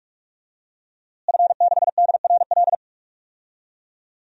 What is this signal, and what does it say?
Morse code, heard as a keyed tone, about 46 wpm, 700 Hz. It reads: F6DRL
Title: DXCC-100 Call Signs - 5 Characters - Repeated 3x - 45wpm